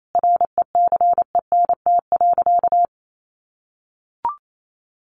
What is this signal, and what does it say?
Morse code, heard as a keyed tone, about 28 wpm, 700 Hz. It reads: RECENT.
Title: Sentences from Top 1400 Words 28wpm